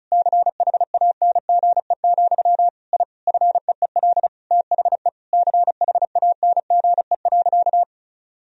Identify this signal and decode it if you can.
Morse code, heard as a keyed tone, about 35 wpm, 700 Hz. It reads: CHANGE, I FEEL THE CHANGE.
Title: Sentences from Top 1000 Words - Review Part 1 - 35wpm